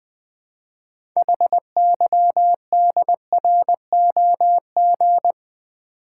Morse code, heard as a keyed tone, about 20 wpm, 700 Hz. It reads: HYDROG
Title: Build A Word - 3 Letters At A Time - 8-Letter Words - 20wpm